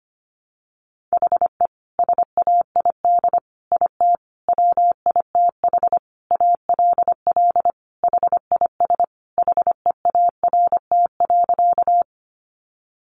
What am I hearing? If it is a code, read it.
Morse code, heard as a keyed tone, about 25 wpm, 700 Hz. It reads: HE SAID IT WITH ALL HIS HEART.
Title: Sentences from Top 1200 Words 25wpm